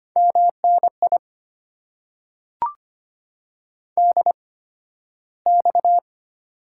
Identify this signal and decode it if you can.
Morse code, heard as a keyed tone, about 25 wpm, spaced as for 13 wpm, 700 Hz. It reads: MNI DX
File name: Top 90 QSO Elements - 8x Character Spacing - 25wpm